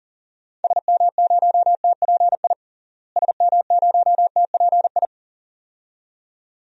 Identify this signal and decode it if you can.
Morse code, heard as a keyed tone, about 40 wpm, 700 Hz. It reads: SM0TPI SM0TPI
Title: DXCC-100 Call Signs - 6 Characters - Repeated 2x - 40wpm